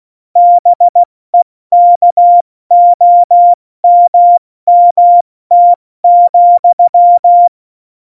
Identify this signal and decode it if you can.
Morse code, heard as a keyed tone, about 16 wpm, 700 Hz. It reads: BEKOMMT,
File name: Morsekurs / Witze / Witz 25. Juni 2019, 70 F80